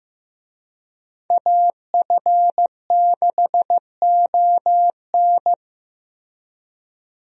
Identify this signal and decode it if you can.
Morse code, heard as a keyed tone, about 16 wpm, 700 Hz. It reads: AF6ON
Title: USA Call Signs - 5 Characters - 15wpm